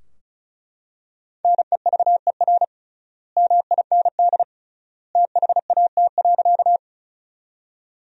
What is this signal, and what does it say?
Morse code, heard as a keyed tone, about 36 wpm, 700 Hz. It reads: NEVER MIND THAT.
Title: 3x Word-Spacing - Sentences from Top 500 Words - 35wpm